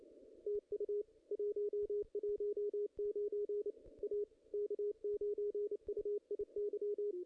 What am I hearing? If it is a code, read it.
Morse code, heard as a keyed tone, about 29 wpm, 410 Hz. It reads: TU 119 A K9UIY
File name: contests / ss-cw-k5tr-op-n5rz -2300